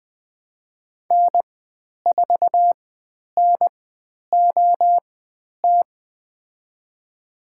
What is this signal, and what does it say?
Morse code, heard as a keyed tone, about 20 wpm, spaced as for 10 wpm, 700 Hz. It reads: N4NOT
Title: 109 Lesson 18g - Letter m - Call Signs 20wpm